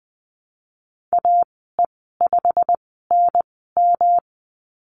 Morse code, heard as a keyed tone, about 20 wpm, spaced as for 15 wpm, 700 Hz. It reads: AE5NM
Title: 213 Lesson 34g - Number 8 - Call Signs 20wpm